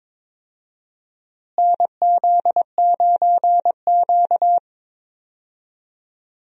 Morse code, heard as a keyed tone, about 22 wpm, 700 Hz. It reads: NZ9Q
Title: USA Call Signs - 4 Characters - Repeated 3x - 22wpm